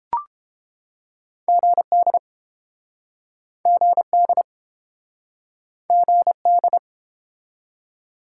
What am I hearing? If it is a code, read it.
Morse code, heard as a keyed tone, about 32 wpm, 700 Hz. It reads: GD GD GD